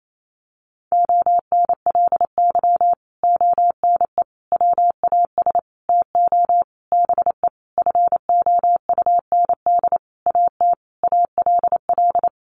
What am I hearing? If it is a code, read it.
Morse code, heard as a keyed tone, about 28 wpm, 700 Hz. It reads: ONLY ONE WAS TO BE FOUND AT ALL
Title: Sentences from Top 1400 Words 28wpm